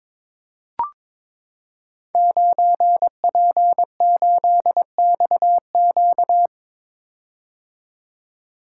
Morse code, heard as a keyed tone, about 22 wpm, 700 Hz. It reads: 9P8XQ
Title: Code Groups - 5 Letter:Numbers - 22wpm